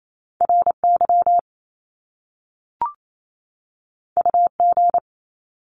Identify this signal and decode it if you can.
Morse code, heard as a keyed tone, about 28 wpm, 700 Hz. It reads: RY UG
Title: Top Two Letter Combinations in Words 28wpm